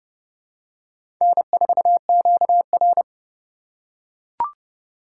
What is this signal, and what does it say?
Morse code, heard as a keyed tone, about 30 wpm, 700 Hz. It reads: N4QR
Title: USA Call Signs - 4 Characters - 4x Character Spacing - 30wpm